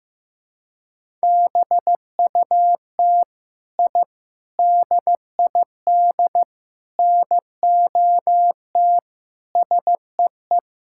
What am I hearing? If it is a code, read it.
Morse code, heard as a keyed tone, about 15 wpm, 700 Hz. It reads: BUT I DID NOT SEE
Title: Sentences from Top 1000 Words - Review Part 5 - 15wpm